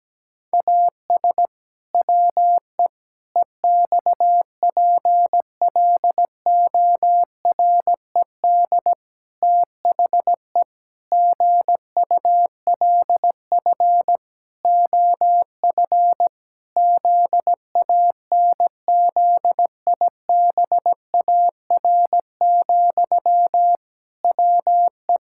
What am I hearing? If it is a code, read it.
Morse code, heard as a keyed tone, about 17 wpm, 700 Hz. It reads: AS WE EXPLORED THE GULF OF ZANZIBAR, WE
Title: Pangram Challenge v2 - 17wpm